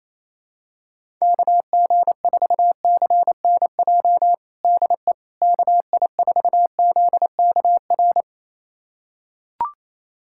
Morse code, heard as a keyed tone, about 28 wpm, 700 Hz. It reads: KG4CNJ DE KI4ZKR
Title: USA Callsign DE Callsign - 6 Characters - 28wpm